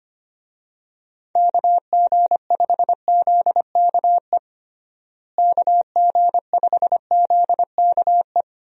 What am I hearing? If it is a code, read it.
Morse code, heard as a keyed tone, about 25 wpm, 700 Hz. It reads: KG5ZKE KG5ZKE
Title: USA Call Signs - 6 Characters - Repeated 2x - 25wpm